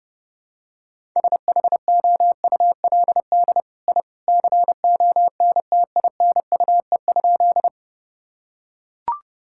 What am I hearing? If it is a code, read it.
Morse code, heard as a keyed tone, about 32 wpm, 700 Hz. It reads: SHOULD I CONTINUE?
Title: Sentences from Top 1000 Words - Review Part 6 - 30wpm